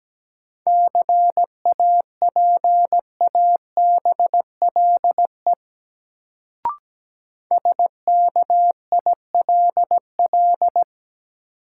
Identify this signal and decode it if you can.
Morse code, heard as a keyed tone, about 17 wpm, 700 Hz. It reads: CAPABLE SKILL
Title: Sentences from Top 1200 Words - New Word Drill - 17wpm